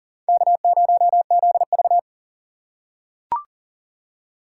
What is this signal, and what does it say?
Morse code, heard as a keyed tone, about 40 wpm, 700 Hz. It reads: K0ZV
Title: USA Call Signs - 4 Characters - 3x Character Spacing - 40wpm